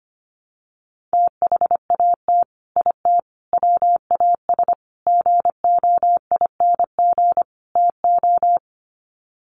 Morse code, heard as a keyed tone, about 25 wpm, 700 Hz. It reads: THAT IT WAS GOING TO
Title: Sets of 5 Words Encore2 25wpm